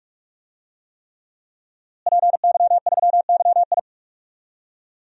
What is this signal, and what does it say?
Morse code, heard as a keyed tone, about 46 wpm, 700 Hz. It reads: PY3YI
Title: DXCC-100 Call Signs - 5 Characters - Repeated 3x - 45wpm